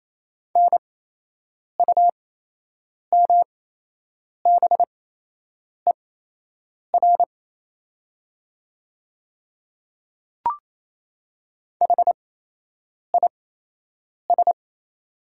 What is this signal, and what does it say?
Morse code, heard as a keyed tone, about 28 wpm, spaced as for 8 wpm, 700 Hz. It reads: NUMBER HIS
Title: 8x Character Spacing - Top 100 Words - 28wpm